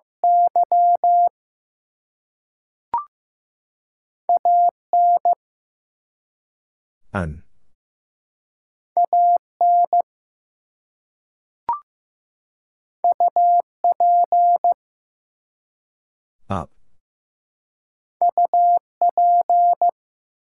Morse code, heard as a keyed tone, about 15 wpm, 700 Hz. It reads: Y AN AN UP UP
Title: Two Letter Words - 15wpm